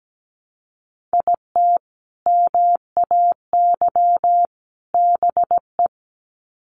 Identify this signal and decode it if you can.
Morse code, heard as a keyed tone, about 17 wpm, 700 Hz. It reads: IT MAY BE
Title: Sets of 3 Words 17wpm